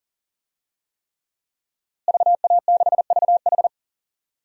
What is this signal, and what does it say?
Morse code, heard as a keyed tone, about 40 wpm, 700 Hz. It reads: VA6VH
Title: DXCC-100 Call Signs - 5 Characters - Repeated 3x - 40wpm